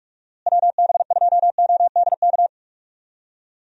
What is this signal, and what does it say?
Morse code, heard as a keyed tone, about 46 wpm, 700 Hz. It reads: WB2ODK
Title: USA Call Signs - 6 Characters - Repeated 2x - 45wpm